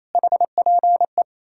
Morse code, heard as a keyed tone, about 37 wpm, 700 Hz. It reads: HPE